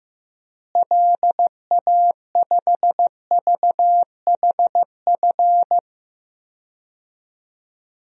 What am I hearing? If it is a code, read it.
Morse code, heard as a keyed tone, about 15 wpm, 700 Hz. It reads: LA5VHF
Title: DXCC-100 Call Signs - 6 Characters - Repeated 3x - 15wpm